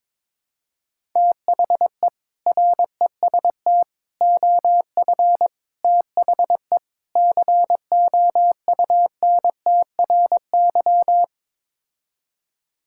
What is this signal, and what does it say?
Morse code, heard as a keyed tone, about 22 wpm, 700 Hz. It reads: THE REST OF THE COUNTRY
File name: practice / 22wpm / Build A Phrase - 5-Word Phrases - 22wpm